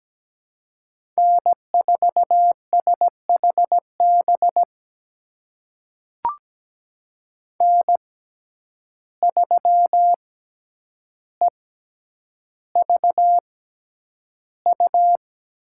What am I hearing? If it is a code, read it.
Morse code, heard as a keyed tone, about 17 wpm, spaced as for 9 wpm, 700 Hz. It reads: N4SHB N3EVU
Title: USA Call Signs - 5 Characters - 6x Character Spacing - 17wpm